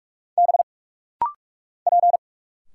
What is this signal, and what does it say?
Morse code, heard as a keyed tone, about 46 wpm, 700 Hz. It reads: B P
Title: Single Letters - Rapid-Fire 45wpm